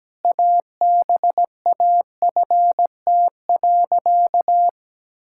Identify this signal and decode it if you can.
Morse code, heard as a keyed tone, about 17 wpm, 700 Hz. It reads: ABAFT.
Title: Pangram Challenge v2 - 17wpm